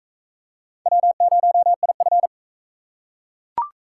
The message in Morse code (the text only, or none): W0IF